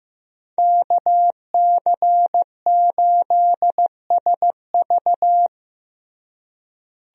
Morse code, heard as a keyed tone, about 16 wpm, 700 Hz. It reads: KC8SV
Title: USA Call Signs - 5 Characters - Repeated 3x - 15wpm